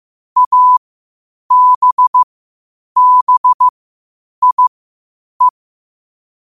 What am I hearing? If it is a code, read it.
Morse code, heard as a keyed tone, about 15 wpm, spaced as for 8 wpm, 1000 Hz. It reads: ABBIE
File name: qotdfiles / 8wpm